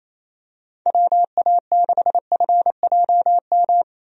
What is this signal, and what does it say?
Morse code, heard as a keyed tone, about 28 wpm, 700 Hz. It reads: WA6FJM
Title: USA Call Signs - 6 Characters - Repeated 2x - 28wpm